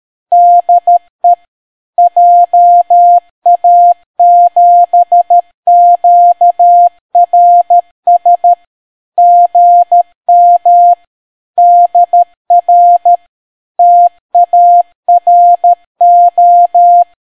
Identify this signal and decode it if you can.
Morse code, heard as a keyed tone, about 13 wpm, 700 Hz. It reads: DE JA7QRS GM DR TARO